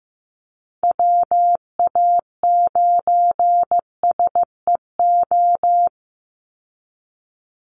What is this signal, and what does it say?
Morse code, heard as a keyed tone, about 15 wpm, 700 Hz. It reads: WA9SEO
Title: Call Signs Repeated 2x - 15wpm